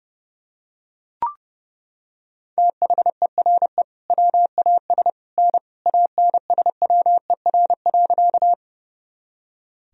Morse code, heard as a keyed tone, about 32 wpm, 700 Hz. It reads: THERE WAS N ANSWER.
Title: Sentences from Top 300 Words and Abbreviations 30wpm